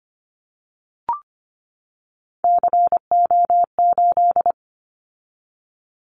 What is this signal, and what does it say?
Morse code, heard as a keyed tone, about 25 wpm, 700 Hz. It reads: CO8
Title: Callsign Prefixes for Top 100 DXCC - 25wpm